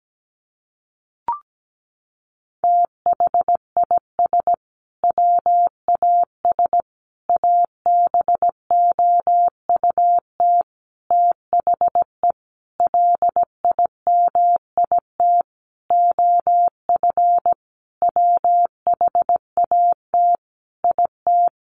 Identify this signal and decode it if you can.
Morse code, heard as a keyed tone, about 17 wpm, 700 Hz. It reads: THIS WAS ABOUT THE LIMIT OF WHAT IT
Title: Sentences from Top 1500 Words 17wpm